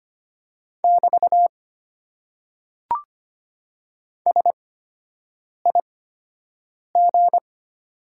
Morse code, heard as a keyed tone, about 25 wpm, spaced as for 7 wpm, 700 Hz. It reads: <BT> SIG
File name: Top 90 QSO Elements - 8x Character Spacing - 25wpm